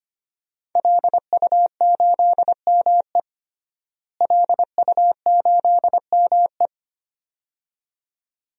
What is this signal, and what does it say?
Morse code, heard as a keyed tone, about 25 wpm, 700 Hz. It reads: LU8ME LU8ME
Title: DXCC-100 Call Signs - 5 Characters - Repeated 2x - 25wpm